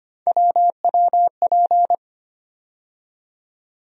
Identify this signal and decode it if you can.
Morse code, heard as a keyed tone, about 25 wpm, 700 Hz. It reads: WWP
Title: USA Call Sign Postfixes - 3 Letters 25wpm